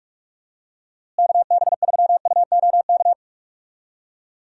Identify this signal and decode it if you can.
Morse code, heard as a keyed tone, about 45 wpm, 700 Hz. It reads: KB3UOK
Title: USA Call Signs - 6 Characters - Repeated 3x - 45wpm